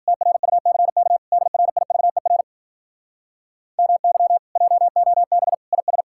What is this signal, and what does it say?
Morse code, heard as a keyed tone, about 52 wpm, 700 Hz. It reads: TRUCK DRIVER MY JOB IS